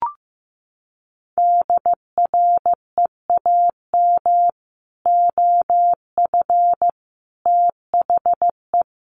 Morse code, heard as a keyed tone, about 15 wpm, 700 Hz. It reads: DREAM OF THE
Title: Sentences from Top 1500 Words 15wpm